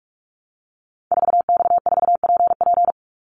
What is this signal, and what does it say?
Morse code, heard as a keyed tone, about 44 wpm, 700 Hz. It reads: 4X4PP
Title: Call Signs - 45wpm